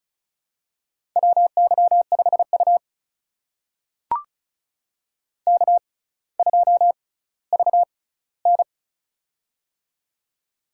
Morse code, heard as a keyed tone, about 33 wpm, spaced as for 18 wpm, 700 Hz. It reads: WY5U K2VN
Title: USA Call Signs - 4 Characters - 6x Character Spacing - 35wpm